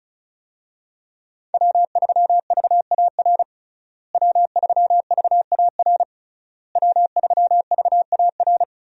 Morse code, heard as a keyed tone, about 36 wpm, 700 Hz. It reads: W3VAR W3VAR W3VAR
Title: USA Call Signs - 5 Characters - Repeated 3x - 35wpm